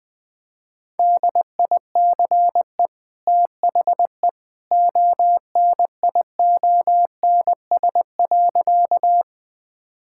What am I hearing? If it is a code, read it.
Morse code, heard as a keyed tone, about 20 wpm, 700 Hz. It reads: DICE THE ONIONS.
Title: Easy Sentences with Hints - 20wpm